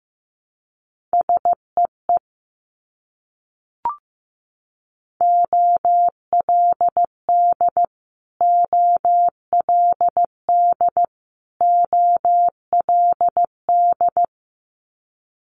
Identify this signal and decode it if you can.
Morse code, heard as a keyed tone, about 15 wpm, 700 Hz. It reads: SEE OLD OLD OLD
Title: Multi-letter Phonemes Repeated 3x 15wpm